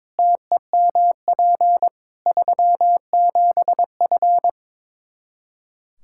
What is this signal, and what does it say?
Morse code, heard as a keyed tone, about 22 wpm, 700 Hz. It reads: TEMP 37F
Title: K1USN SST QSO Practice 22wpm